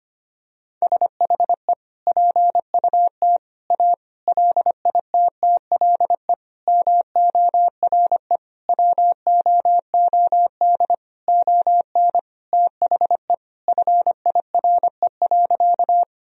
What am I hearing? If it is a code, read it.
Morse code, heard as a keyed tone, about 25 wpm, 700 Hz. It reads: SHE PUT A LITTLE MORE WOOD ON THE FIRE.
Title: Sentences from Top 1000 Words - Review Part 2 - 25wpm